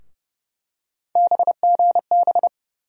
Morse code, heard as a keyed tone, about 30 wpm, 700 Hz. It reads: BGB
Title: USA Call Sign Postfixes - 3 Letters 30wpm